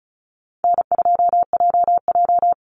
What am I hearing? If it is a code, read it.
Morse code, heard as a keyed tone, about 34 wpm, 700 Hz. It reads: N2JJ
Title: Easy Call Signs Repeated 3x - 35wpm